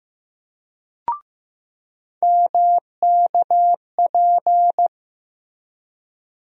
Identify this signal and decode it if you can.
Morse code, heard as a keyed tone, about 15 wpm, 700 Hz. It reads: MKP
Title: USA Call Sign Postfixes - 3 Letters 15wpm